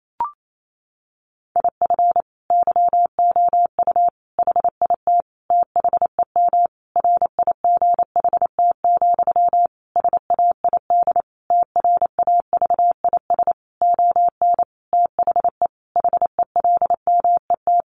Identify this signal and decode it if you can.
Morse code, heard as a keyed tone, about 28 wpm, 700 Hz. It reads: IF YOU HIT THEM RIGHT, SAID TRAVIS ON THE HELMET
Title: A Sound Of Thunder 28wpm